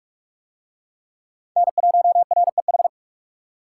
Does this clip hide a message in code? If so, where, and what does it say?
Yes: Morse code from 1.6 to 2.9 s, heard as a keyed tone, about 44 wpm, 700 Hz. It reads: N1REH